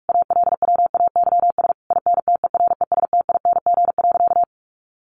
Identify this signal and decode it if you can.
Morse code, heard as a keyed tone, about 44 wpm, 700 Hz. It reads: ALWAYS INTERESTING.
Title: Sentences from Top 1200 Words 45wpm